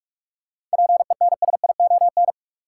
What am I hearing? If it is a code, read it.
Morse code, heard as a keyed tone, about 46 wpm, 700 Hz. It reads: PENSION
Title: Seven Letter Words - 45wpm